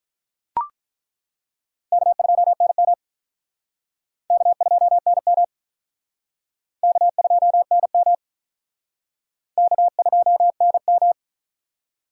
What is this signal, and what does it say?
Morse code, heard as a keyed tone, about 48 wpm, 700 Hz. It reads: K2NM K2NM K2NM K2NM